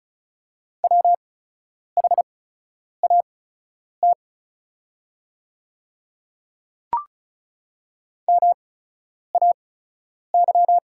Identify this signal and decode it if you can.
Morse code, heard as a keyed tone, about 36 wpm, spaced as for 10 wpm, 700 Hz. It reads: WHAT MAY
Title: 8x Character Spacing - Top 100 Words - 35wpm